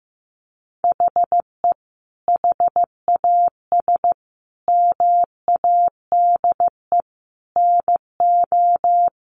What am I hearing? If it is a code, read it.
Morse code, heard as a keyed tone, about 15 wpm, 700 Hz. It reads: HE HAS MADE NO
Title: Sentences from Top 1200 Words 15wpm